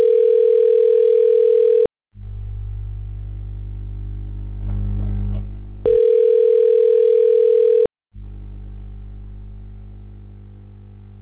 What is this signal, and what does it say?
Ringback tone (North America), 440 + 480 Hz.